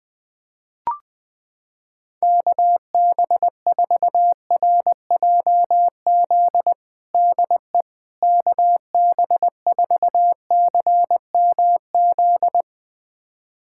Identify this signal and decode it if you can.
Morse code, heard as a keyed tone, about 20 wpm, 700 Hz. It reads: KB4RJZ DE KB4CMZ
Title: USA Callsign DE Callsign - 6 Characters - 20wpm